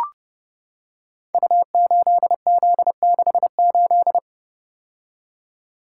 Morse code, heard as a keyed tone, about 30 wpm, 700 Hz. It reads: U8Z68
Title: Code Groups - 5 Letter:Numbers - 30wpm